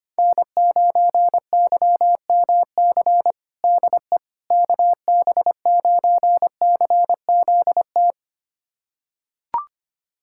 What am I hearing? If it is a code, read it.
Morse code, heard as a keyed tone, about 25 wpm, 700 Hz. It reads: N9YMC DE KB9CZT
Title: USA Callsign DE Callsign 25wpm